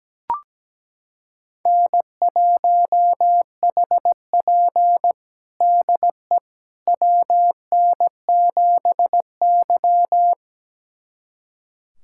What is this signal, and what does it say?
Morse code, heard as a keyed tone, about 17 wpm, 700 Hz. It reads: N1HP DE WN7Y
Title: USA Callsign DE Callsign - 4 Characters - 17wpm